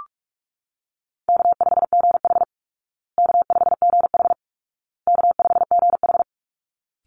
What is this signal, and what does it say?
Morse code, heard as a keyed tone, about 44 wpm, 700 Hz. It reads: K5GH K5GH K5GH
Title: Call Signs Repeated 2x - 45wpm